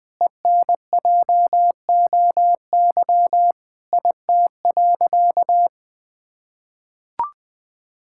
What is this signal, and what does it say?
Morse code, heard as a keyed tone, about 21 wpm, 700 Hz. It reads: ENJOY IT.
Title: Sentences from Top 1000 Words - Review Part 5 - 20wpm